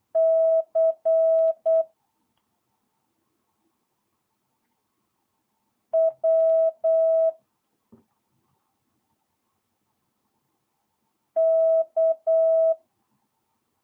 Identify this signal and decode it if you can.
Morse code, heard as a keyed tone, about 8 wpm, 650 Hz. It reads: C W K
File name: jh8cbh / 2022cw